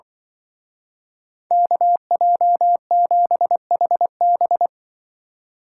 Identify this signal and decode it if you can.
Morse code, heard as a keyed tone, about 23 wpm, 700 Hz. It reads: KJ7HB
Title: Speed Racing - USA 2x2 Call Signs 20wpm